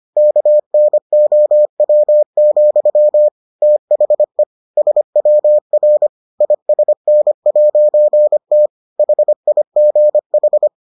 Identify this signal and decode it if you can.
Morse code, heard as a keyed tone, about 25 wpm, 600 Hz. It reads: KNOW, THE SWR ISN'T HIGH